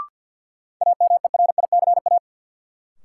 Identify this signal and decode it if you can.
Morse code, heard as a keyed tone, about 47 wpm, 700 Hz. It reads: AMERICA
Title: Sets of 5 Words - New Word Drill - 50wpm